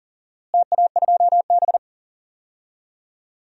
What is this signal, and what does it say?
Morse code, heard as a keyed tone, about 40 wpm, 700 Hz. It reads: TA2B
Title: DXCC-100 Call Signs - 4 Characters - Repeated 2x - 40wpm